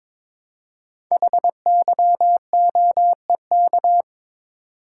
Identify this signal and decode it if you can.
Morse code, heard as a keyed tone, about 22 wpm, 700 Hz. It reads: HYOEK
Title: Code Groups - 5 Letters - 22wpm